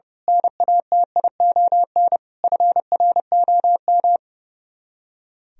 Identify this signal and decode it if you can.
Morse code, heard as a keyed tone, about 30 wpm, 700 Hz. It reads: NATION FROM
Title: Speed-Racing - Sets of 5 Words 30wpm